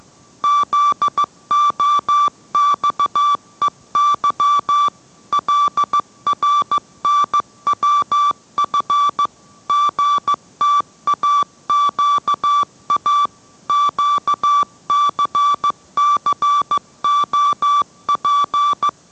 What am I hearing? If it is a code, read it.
Morse code, heard as a keyed tone, about 15 wpm, 1160 Hz. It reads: ZOXEY LRNWF GTAQA QCCOP